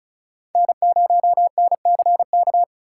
Morse code, heard as a keyed tone, about 36 wpm, 700 Hz. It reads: N0NCK
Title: USA Call Signs - 5 Characters - 3x Character Spacing - 35wpm